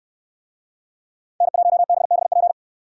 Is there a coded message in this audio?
Yes: Morse code from 1.4 to 2.5 s, heard as a keyed tone, about 68 wpm, 700 Hz. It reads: N1BBP